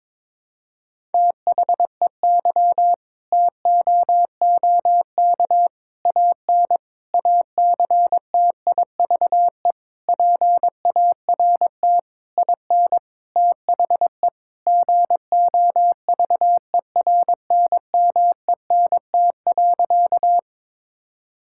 Morse code, heard as a keyed tone, about 22 wpm, 700 Hz. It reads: THEY TOOK AN ACTIVE PART IN THE GOVERNMENT.
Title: Sentences from Top 1000 Words - Review Part 6 - 22wpm